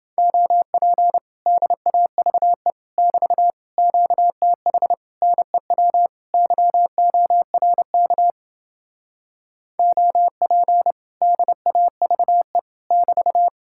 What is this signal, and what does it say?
Morse code, heard as a keyed tone, about 29 wpm, 700 Hz. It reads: OP DAVE <BT> QTH NEW YORK OP DAVE <BT>